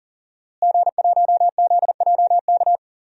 Morse code, heard as a keyed tone, about 37 wpm, 700 Hz. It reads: G1ZJK
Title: DXCC-100 Call Signs - 5 Characters - Repeated 3x - 40wpm